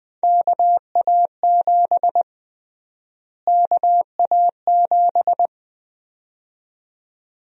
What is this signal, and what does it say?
Morse code, heard as a keyed tone, about 20 wpm, 700 Hz. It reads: KA7 KA7
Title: USA Call Sign Prefixes - Repeated 2x - 20wpm